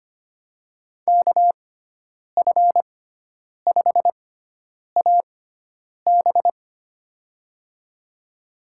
Morse code, heard as a keyed tone, about 25 wpm, spaced as for 9 wpm, 700 Hz. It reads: KF5AB
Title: USA Call Signs - 5 Characters - 6x Character Spacing - 25wpm